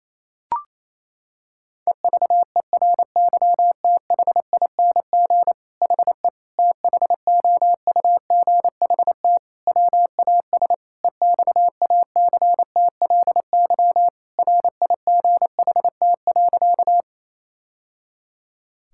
Morse code, heard as a keyed tone, about 28 wpm, 700 Hz. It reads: EVERYTHING HE THOUGHT WAS EXACTLY RIGHT.
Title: Sentences from Top 1000 Words - Review Part 6 - 28wpm